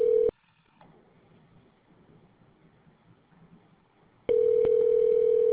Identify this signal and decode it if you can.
Ringback tone (North America), 440 + 480 Hz.